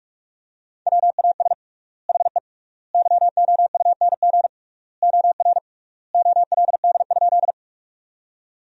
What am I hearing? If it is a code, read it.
Morse code, heard as a keyed tone, about 46 wpm, 700 Hz. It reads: WAS HE YOUNG OR OLD?
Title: 3x Word-Spacing - Sentences from Top 400 Words - 45wpm